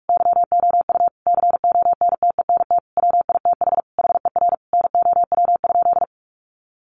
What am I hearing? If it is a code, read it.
Morse code, heard as a keyed tone, about 44 wpm, 700 Hz. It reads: YOU CONTENT WITH HER NOW?
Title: Sentences from Top 900 Words 45wpm